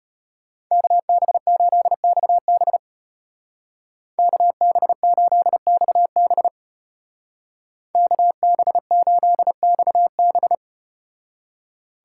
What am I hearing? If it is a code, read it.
Morse code, heard as a keyed tone, about 32 wpm, 700 Hz. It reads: KB8XB KB8XB KB8XB